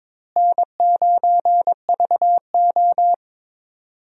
Morse code, heard as a keyed tone, about 22 wpm, 700 Hz. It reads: N9VO
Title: USA Call Signs - 4 Characters - Repeated 2x - 22wpm